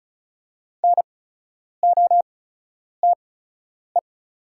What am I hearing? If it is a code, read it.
Morse code, heard as a keyed tone, about 34 wpm, spaced as for 10 wpm, 700 Hz. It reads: NOTE
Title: 8x Character Spacing - Top 500 Words - 35wpm